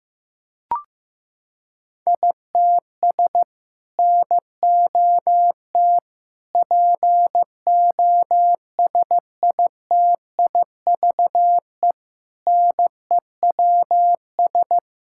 Morse code, heard as a keyed tone, about 16 wpm, 700 Hz. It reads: ITS NOT POSITIVE NEWS
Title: Sentences from Top 1000 Words - Review Part 5 - 15wpm